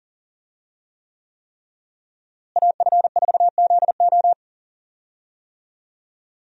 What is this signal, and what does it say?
Morse code, heard as a keyed tone, about 40 wpm, 700 Hz. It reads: AF4ZO
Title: USA Call Signs - 5 Characters - Repeated 3x - 40wpm